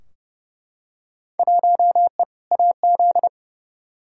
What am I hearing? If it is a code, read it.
Morse code, heard as a keyed tone, about 30 wpm, 700 Hz. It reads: 1E AZ
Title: Field Day Exchanges - 30wpm